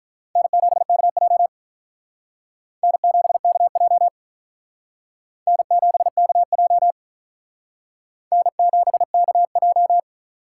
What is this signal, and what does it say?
Morse code, heard as a keyed tone, about 45 wpm, 700 Hz. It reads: N7KJ N7KJ N7KJ N7KJ